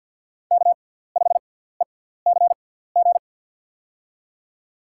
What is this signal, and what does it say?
Morse code, heard as a keyed tone, about 52 wpm, spaced as for 17 wpm, 700 Hz. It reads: K5ECG
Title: USA Call Signs - 5 Characters - 6x Character Spacing - 50wpm